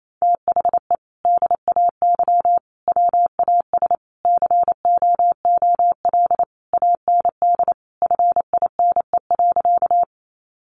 Morse code, heard as a keyed tone, about 28 wpm, 700 Hz. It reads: THE DAY WAS COOL AND FINE.
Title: Sentences from Top 1200 Words 28wpm